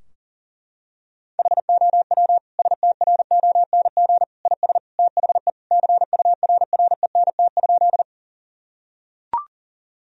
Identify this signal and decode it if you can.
Morse code, heard as a keyed tone, about 40 wpm, 700 Hz. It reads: HOW STRONG IS THE CURRENT?
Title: Sentences from Top 1000 Words - Review Part 4 - 40wpm